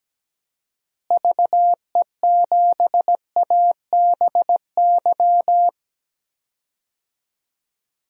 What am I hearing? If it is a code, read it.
Morse code, heard as a keyed tone, about 17 wpm, 700 Hz. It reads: VE7ABY
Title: DXCC-100 Call Signs - 6 Characters - Repeated 2x - 17wpm